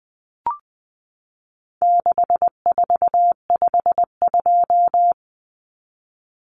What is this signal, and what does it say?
Morse code, heard as a keyed tone, about 20 wpm, 700 Hz. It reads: 6452
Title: Common Cut Numbers - 4-Digit 20wpm